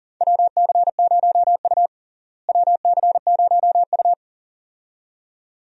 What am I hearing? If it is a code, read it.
Morse code, heard as a keyed tone, about 40 wpm, 700 Hz. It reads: WC0U WC0U
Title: USA Call Signs - 4 Characters - Repeated 2x - 40wpm